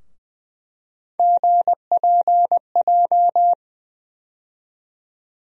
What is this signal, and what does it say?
Morse code, heard as a keyed tone, about 20 wpm, 700 Hz. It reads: GPJ
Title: USA Call Sign Postfixes - 3 Letters 20wpm